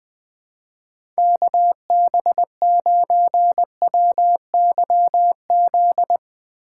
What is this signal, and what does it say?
Morse code, heard as a keyed tone, about 20 wpm, 700 Hz. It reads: KB9WYZ
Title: USA Call Signs - 2x Character Spacing - 20wpm